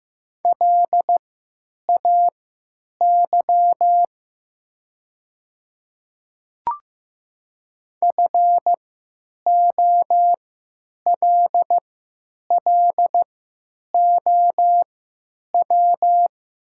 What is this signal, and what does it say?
Morse code, heard as a keyed tone, about 15 wpm, spaced as for 9 wpm, 700 Hz. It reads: LAY FOLLOW
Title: 3x Character Spacing - Top 500 Words - 15wpm